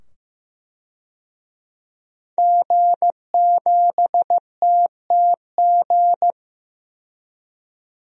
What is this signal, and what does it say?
Morse code, heard as a keyed tone, about 15 wpm, 700 Hz. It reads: G7TTG
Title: DXCC-100 Call Signs - 5 Characters - 15wpm